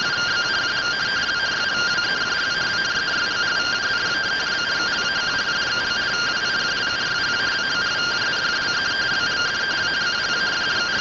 Two-tone teleprinter signal, about 170 Hz shift.
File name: findings / 100 surview 0.gen-10 rtty rx